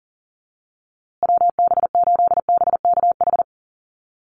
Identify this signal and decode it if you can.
Morse code, heard as a keyed tone, about 40 wpm, 700 Hz. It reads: WB8BKH
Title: USA Call Signs Repeated 3x 40wpm